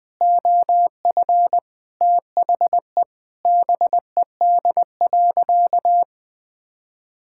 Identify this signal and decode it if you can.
Morse code, heard as a keyed tone, about 20 wpm, 700 Hz. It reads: OF THE BED.
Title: Sentences with 3-Word Phrases - 20wpm